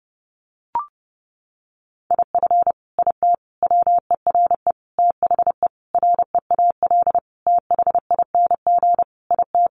Morse code, heard as a keyed tone, about 30 wpm, 700 Hz. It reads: IF IT WERE THE REAL THING IT
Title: Sentences from Top 1300 Words 30wpm